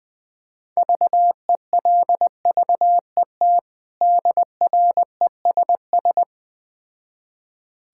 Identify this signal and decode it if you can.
Morse code, heard as a keyed tone, about 20 wpm, 700 Hz. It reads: VELVET DRESS
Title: Difficult Words with Hints - 20wpm